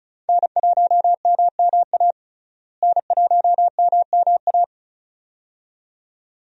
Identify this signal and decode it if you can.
Morse code, heard as a keyed tone, about 34 wpm, 700 Hz. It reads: N1MMA N1MMA
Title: USA Call Signs - 5 Characters - Repeated 2x - 35wpm